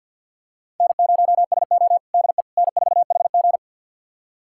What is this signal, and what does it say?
Morse code, heard as a keyed tone, about 53 wpm, 700 Hz. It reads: N0SO DE N4HG
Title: USA Callsign DE Callsign - 4 Characters - 50wpm